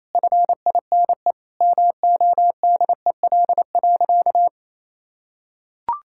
Morse code, heard as a keyed tone, about 28 wpm, 700 Hz. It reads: FINE MODEL.
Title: Sentences from Top 1000 Words - Review Part 5 - 28wpm